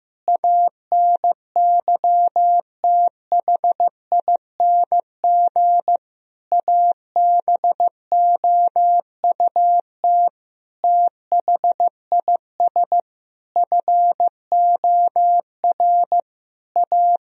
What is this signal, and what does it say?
Morse code, heard as a keyed tone, about 15 wpm, 700 Hz. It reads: ANYTHING ABOUT THIS FOR A
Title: Sentences from Top 1000 Words - Review Part 1 - 15wpm